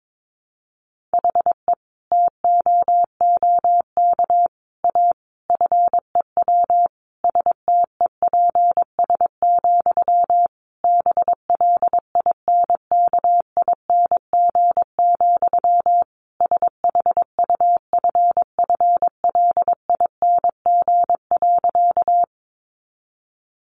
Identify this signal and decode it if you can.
Morse code, heard as a keyed tone, about 22 wpm, 700 Hz. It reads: HE TOOK A FEW STEPS, BLINKING, SHUFFLING.
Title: A Sound Of Thunder 22wpm